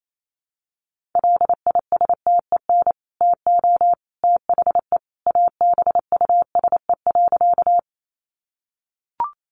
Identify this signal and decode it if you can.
Morse code, heard as a keyed tone, about 28 wpm, 700 Hz. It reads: LISTEN TO THE ABUSE.
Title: Sentences from Top 1300 Words 28wpm